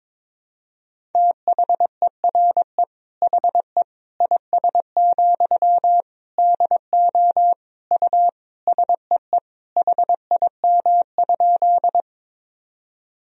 Morse code, heard as a keyed tone, about 22 wpm, 700 Hz. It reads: THERE HE IS, DO U SEE HIM?
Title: Sentences from Top 100 Words and Abbreviations 22wpm